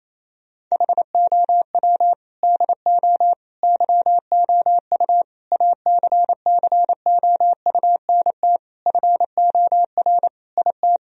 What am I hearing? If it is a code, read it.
Morse code, heard as a keyed tone, about 27 wpm, 700 Hz. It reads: HOW DO YOU ACCOUNT FOR IT
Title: Sentences from Top 1000 Words - Review Part 3 - 28wpm